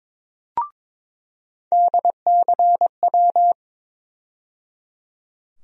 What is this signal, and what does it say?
Morse code, heard as a keyed tone, about 22 wpm, 700 Hz. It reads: DCW
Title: USA Call Sign Postfixes - 3 Letters 22wpm